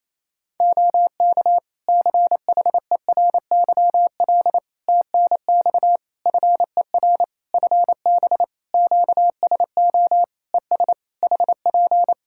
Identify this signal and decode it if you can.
Morse code, heard as a keyed tone, about 28 wpm, 700 Hz. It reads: OK CHERYL TNX FER FB QSO ES HP
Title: Practice QSO - Exchange 4 - Ending - 28wpm